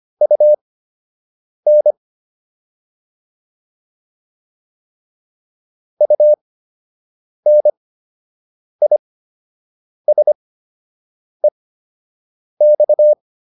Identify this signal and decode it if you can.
Morse code, heard as a keyed tone, about 25 wpm, spaced as for 7 wpm, 600 Hz. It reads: UN UNISEX